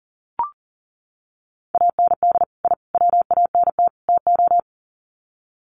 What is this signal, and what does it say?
Morse code, heard as a keyed tone, about 40 wpm, 700 Hz. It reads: AND I WANT TO
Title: Sets of 4 Words 40wpm